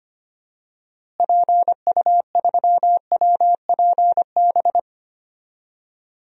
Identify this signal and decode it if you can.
Morse code, heard as a keyed tone, about 25 wpm, 700 Hz. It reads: PU3WPB
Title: DXCC-100 Call Signs - 6 Characters - Repeated 3x - 25wpm